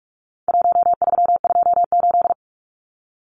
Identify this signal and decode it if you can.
Morse code, heard as a keyed tone, about 44 wpm, 700 Hz. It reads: 1328
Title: Numbers - 4-Digit 45wpm